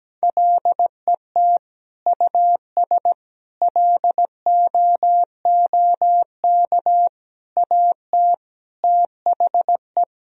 Morse code, heard as a keyed tone, about 17 wpm, 700 Hz. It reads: LET US LOOK AT THE
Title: Sentences from Top 1000 Words - Review Part 2 - 17wpm